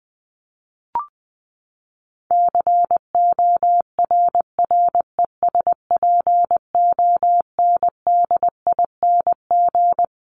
Morse code, heard as a keyed tone, about 20 wpm, 700 Hz. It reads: CORRESPONDING
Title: ING Suffix - Top 100 Words with Base Words 20wpm